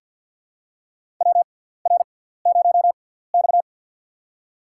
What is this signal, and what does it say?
Morse code, heard as a keyed tone, about 53 wpm, spaced as for 17 wpm, 700 Hz. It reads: WR0X
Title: USA Call Signs - 6x Character Spacing - 50wpm